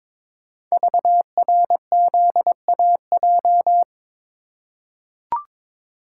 Morse code, heard as a keyed tone, about 23 wpm, 690 Hz. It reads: VRZAJ
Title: Code Groups - 5 Letters - 22wpm